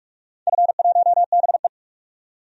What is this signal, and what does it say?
Morse code, heard as a keyed tone, about 41 wpm, 700 Hz. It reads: F1BE
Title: DXCC-100 Call Signs - 4 Characters - 45wpm